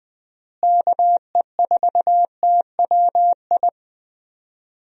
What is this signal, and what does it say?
Morse code, heard as a keyed tone, about 20 wpm, 700 Hz. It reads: KE4TWI
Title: USA Call Signs - 6 Characters - Repeated 3x - 20wpm